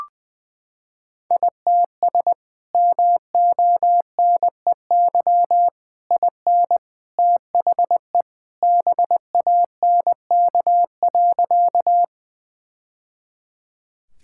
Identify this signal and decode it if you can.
Morse code, heard as a keyed tone, about 21 wpm, 700 Hz. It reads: ITS MONEY IN THE BANK.
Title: Sentences from Top 1000 Words - Review Part 6 - 20wpm